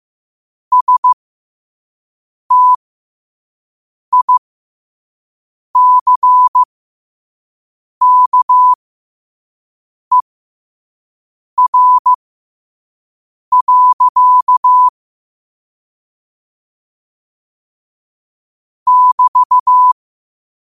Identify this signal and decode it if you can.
Morse code, heard as a keyed tone, about 15 wpm, spaced as for 5 wpm, 1000 Hz. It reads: STICKER. <BT>